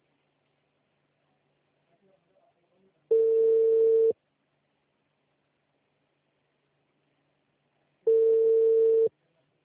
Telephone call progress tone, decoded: ringback tone